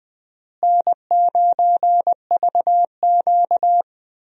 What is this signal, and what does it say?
Morse code, heard as a keyed tone, about 21 wpm, 700 Hz. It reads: N9VQ
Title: USA Call Signs - 4 Characters - Repeated 3x - 20wpm